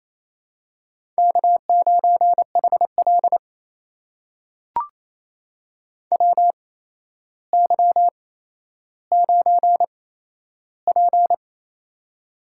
Morse code, heard as a keyed tone, about 29 wpm, spaced as for 12 wpm, 700 Hz. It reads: K9HL WY9P
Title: USA Call Signs - 4 Characters - 8x Character Spacing - 28wpm